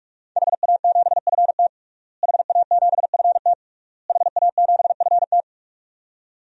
Morse code, heard as a keyed tone, about 45 wpm, 700 Hz. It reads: HA7FT HA7FT HA7FT
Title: DXCC-100 Call Signs - 5 Characters - Repeated 3x - 45wpm